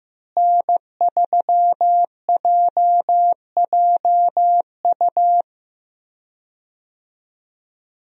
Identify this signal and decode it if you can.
Morse code, heard as a keyed tone, about 16 wpm, 700 Hz. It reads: N3JJU
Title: USA Call Signs - 5 Characters - Repeated 2x - 15wpm